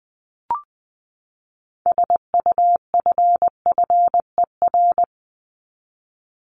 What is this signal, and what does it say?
Morse code, heard as a keyed tone, about 20 wpm, 700 Hz. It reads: SUFFER
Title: Top 1501-1750 Words 20wpm